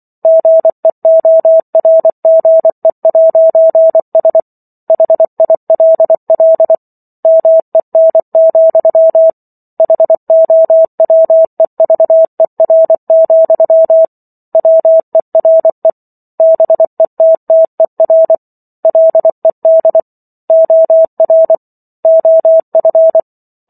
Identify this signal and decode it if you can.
Morse code, heard as a keyed tone, about 24 wpm, 650 Hz. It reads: GEORGE'S HILL MEN, HOWEVER, WERE BETTER LED OR OF